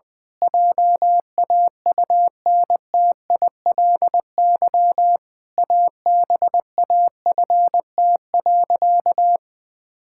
Morse code, heard as a keyed tone, about 20 wpm, 700 Hz. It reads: JAUNTILY ABAFT.
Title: Pangram Challenge v2 - 20wpm